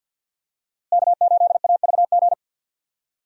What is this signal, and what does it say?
Morse code, heard as a keyed tone, about 47 wpm, 700 Hz. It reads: K8AVG
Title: USA Call Signs - 5 Characters - Repeated 3x - 50wpm